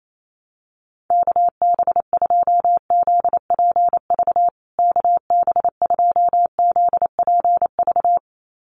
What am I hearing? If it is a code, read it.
Morse code, heard as a keyed tone, about 28 wpm, 700 Hz. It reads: KB2ZPV KB2ZPV
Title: USA Call Signs Repeated 2x 28wpm